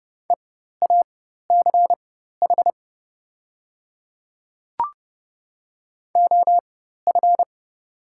Morse code, heard as a keyed tone, about 30 wpm, spaced as for 14 wpm, 700 Hz. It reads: EACH OF
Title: 4x Character Spacing - Top 100 Words - 30wpm